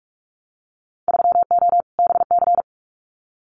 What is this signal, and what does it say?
Morse code, heard as a keyed tone, about 44 wpm, 690 Hz. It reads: 3O BC
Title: Winter Field Day Exchanges Encore 45wpm